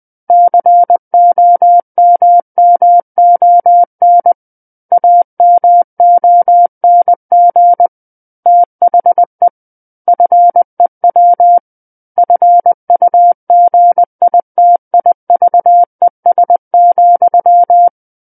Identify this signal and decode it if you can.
Morse code, heard as a keyed tone, about 20 wpm, 700 Hz. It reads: COMMON AMONG THE FEW FUGITIVES,
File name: 20 WPM